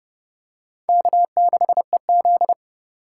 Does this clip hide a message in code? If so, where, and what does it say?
Yes: Morse code from 0.9 to 2.5 s, heard as a keyed tone, about 27 wpm, 700 Hz. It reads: K6EZ